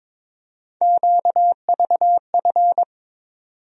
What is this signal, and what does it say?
Morse code, heard as a keyed tone, about 22 wpm, 700 Hz. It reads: QVF
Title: USA Call Sign Postfixes - 3 Letters 22wpm